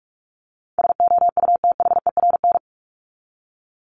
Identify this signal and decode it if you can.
Morse code, heard as a keyed tone, about 44 wpm, 700 Hz. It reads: SOUTHERN
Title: Top 1751-2000 Words 45wpm